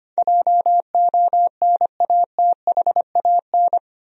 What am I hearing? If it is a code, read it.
Morse code, heard as a keyed tone, about 25 wpm, 700 Hz. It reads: JONATHAN
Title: USA Common Names - 25wpm